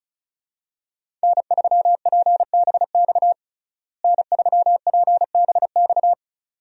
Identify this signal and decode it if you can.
Morse code, heard as a keyed tone, about 36 wpm, 700 Hz. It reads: N3PBX N3PBX
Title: USA Call Signs - 5 Characters - Repeated 2x - 35wpm